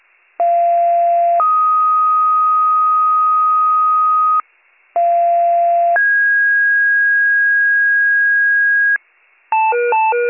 Two-tone page, heard as a one-second tone, then a three-second tone, about 690 then 1250 Hz, about 690 then 1650 Hz, with noise. It is heard from 0.4 to 9.0 s.